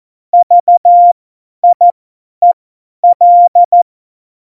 Morse code, heard as a keyed tone, about 14 wpm, spaced as for 10 wpm, 700 Hz. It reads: VIEL
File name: Morsekurs / Witze / Witz 16. Oktober 2018, 50 F70